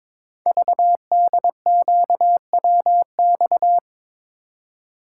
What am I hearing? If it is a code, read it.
Morse code, heard as a keyed tone, about 23 wpm, 700 Hz. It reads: VDQWX
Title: Code Groups - 5 Letter:Numbers - 22wpm